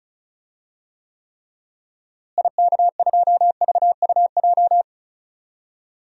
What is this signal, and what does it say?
Morse code, heard as a keyed tone, about 36 wpm, 700 Hz. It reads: IK2VUJ
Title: DXCC-100 Call Signs - 6 Characters - Repeated 3x - 35wpm